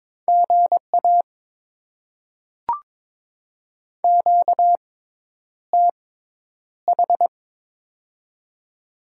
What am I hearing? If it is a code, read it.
Morse code, heard as a keyed tone, about 22 wpm, spaced as for 10 wpm, 700 Hz. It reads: GA QTH
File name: Top 20 QSO Elements - 6x Character Spacing - 22wpm